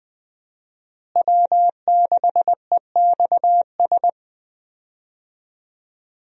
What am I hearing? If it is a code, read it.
Morse code, heard as a keyed tone, about 20 wpm, 700 Hz. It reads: W6EXS
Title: USA Call Signs - 5 Characters - Repeated 3x - 20wpm